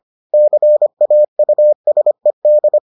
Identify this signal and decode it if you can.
Morse code, heard as a keyed tone, about 25 wpm, 600 Hz. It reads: CAUSED